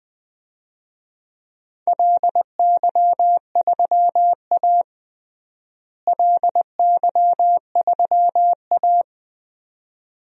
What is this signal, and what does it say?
Morse code, heard as a keyed tone, about 20 wpm, 700 Hz. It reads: LY3A LY3A
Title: DXCC-100 Call Signs - 4 Characters - Repeated 2x - 20wpm